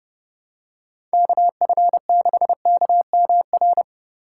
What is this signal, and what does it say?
Morse code, heard as a keyed tone, about 30 wpm, 700 Hz. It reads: KF6KMR
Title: USA Call Signs - 6 Characters - 3x Character Spacing - 30wpm